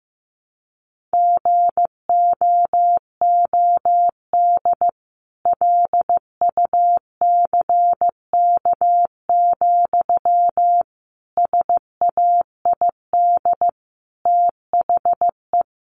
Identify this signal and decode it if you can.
Morse code, heard as a keyed tone, about 15 wpm, 700 Hz. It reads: GOOD LUCK, SAID THE
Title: A Sound Of Thunder 15wpm